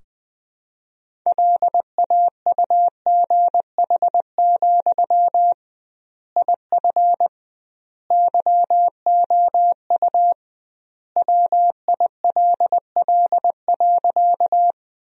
Morse code, heard as a keyed tone, about 20 wpm, 700 Hz. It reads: LAUGH, IF YOU WILL.
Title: 2x Word-Spacing - Sentences from Top 500 Words - 20wpm